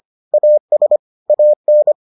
Morse code, heard as a keyed tone, about 25 wpm, 600 Hz. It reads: AS AN